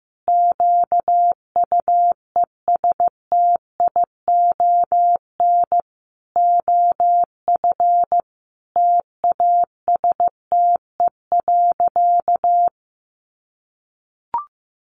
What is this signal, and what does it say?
Morse code, heard as a keyed tone, about 15 wpm, 700 Hz. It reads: QUESTION OF TASTE.
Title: Sentences from Top 1100 Words 15wpm